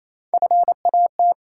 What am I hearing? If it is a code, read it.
Morse code, heard as a keyed tone, about 27 wpm, 700 Hz. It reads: FAT
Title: Three Letter Words - 28wpm